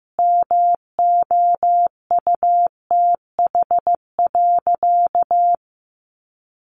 Morse code, heard as a keyed tone, about 15 wpm, 700 Hz. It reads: MOUTH.
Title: Sentences from Top 1500 Words 15wpm